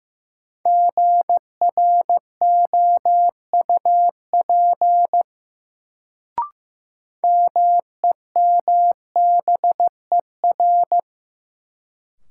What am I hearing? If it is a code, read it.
Morse code, heard as a keyed tone, about 15 wpm, 700 Hz. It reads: GROUP MEMBER
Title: Sets of 3 Words - New Word Drill - 15wpm